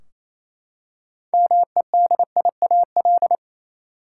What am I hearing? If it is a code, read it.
Morse code, heard as a keyed tone, about 28 wpm, 700 Hz. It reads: MEDIAL
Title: Practice QSO - Exchange 3 - More Info - 28wpm